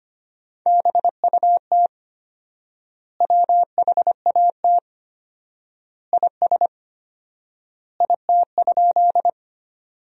Morse code, heard as a keyed tone, about 25 wpm, 700 Hz. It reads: BUT WHAT IS IT?
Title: 4x Word-Spacing - Sentences from Top 100 Words - 25wpm